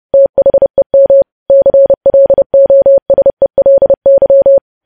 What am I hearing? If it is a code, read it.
Morse code, heard as a keyed tone, about 30 wpm, 550 Hz. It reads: THEM CLOSELY
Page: news headlines in morse code at 30 wpm